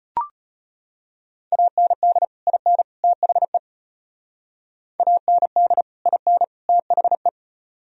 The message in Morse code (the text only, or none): AND IN THE AND IN THE